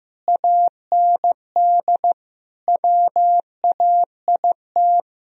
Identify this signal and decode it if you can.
Morse code, heard as a keyed tone, about 15 wpm, 700 Hz. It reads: AND WAIT
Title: Sentences from Top 1000 Words - Review Part 1 - 15wpm